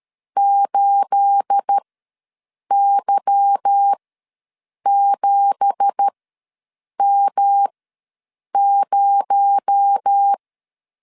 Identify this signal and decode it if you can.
Morse code, heard as a keyed tone, about 13 wpm, spaced as for 7 wpm, 800 Hz. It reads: 8Y7M0
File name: LEZIONI / 9 2867 ws 12wpm 3431